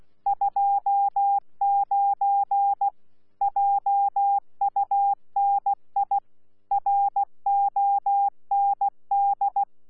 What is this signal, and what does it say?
Morse code, heard as a keyed tone, about 16 wpm, 800 Hz. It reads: 29 JUNI ROND
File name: Z PI4VRZ / PI4VRZ 14-jun-2025